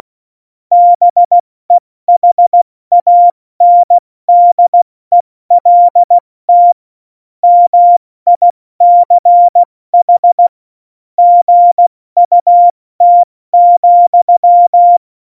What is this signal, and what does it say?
Morse code, heard as a keyed tone, about 16 wpm, 700 Hz. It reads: BEHANDELT MICH GUT,
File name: Morsekurs / Witze / Witz 06. Oktober 2020, 70 F80